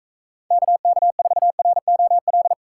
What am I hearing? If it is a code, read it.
Morse code, heard as a keyed tone, about 44 wpm, 700 Hz. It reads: KK4ROL